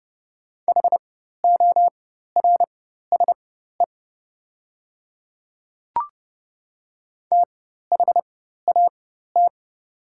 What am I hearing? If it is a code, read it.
Morse code, heard as a keyed tone, about 31 wpm, spaced as for 14 wpm, 700 Hz. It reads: HORSE THAT
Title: 4x Character Spacing - Top 300 Words - 30wpm